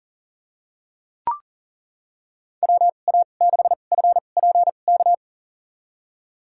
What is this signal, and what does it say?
Morse code, heard as a keyed tone, about 37 wpm, spaced as for 28 wpm, 700 Hz. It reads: WA6FPK
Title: USA Call Signs - 2x Character Spacing - 40wpm